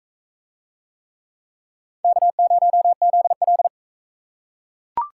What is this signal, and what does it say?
Morse code, heard as a keyed tone, about 40 wpm, 700 Hz. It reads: K0ZL